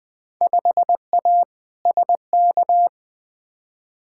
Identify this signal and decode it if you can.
Morse code, heard as a keyed tone, about 20 wpm, 700 Hz. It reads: 5A SK
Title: Field Day Exchanges Encore - 20wpm